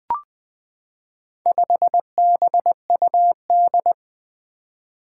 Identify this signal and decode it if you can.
Morse code, heard as a keyed tone, about 20 wpm, 700 Hz. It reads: 5BUD
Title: Four Challenging Letter-Number Combinations 20wpm